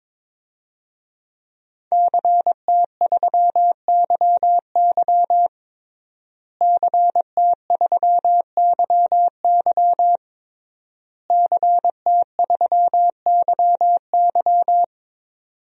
Morse code, heard as a keyed tone, about 21 wpm, 700 Hz. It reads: CT3YY CT3YY CT3YY
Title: DXCC-100 Call Signs - 5 Characters - Repeated 3x - 22wpm